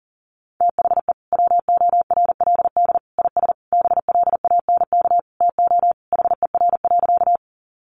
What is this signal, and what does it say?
Morse code, heard as a keyed tone, about 40 wpm, 700 Hz. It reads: THE WORLD IS BLANK TO HER.
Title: Sentences from Top 1500 Words 40wpm